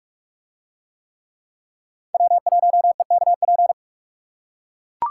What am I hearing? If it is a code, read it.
Morse code, heard as a keyed tone, about 41 wpm, 700 Hz. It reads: W1EKP